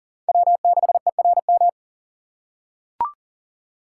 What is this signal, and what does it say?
Morse code, heard as a keyed tone, about 40 wpm, 700 Hz. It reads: W6ERM
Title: USA Call Signs - 5 Characters - 4x Character Spacing - 40wpm